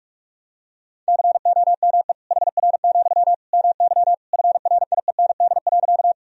Morse code, heard as a keyed tone, about 45 wpm, 700 Hz. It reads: COME HR, MY FRIEND.
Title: Sentences from Top 300 Words and Abbreviations 45wpm